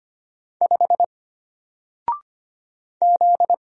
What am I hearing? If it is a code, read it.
Morse code, heard as a keyed tone, about 25 wpm, 700 Hz. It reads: 5 Z
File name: Single-Letter-Number-25wpm